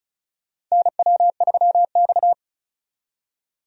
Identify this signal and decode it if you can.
Morse code, heard as a keyed tone, about 34 wpm, 700 Hz. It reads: NW3X
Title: USA Call Signs - 4 Characters - 35wpm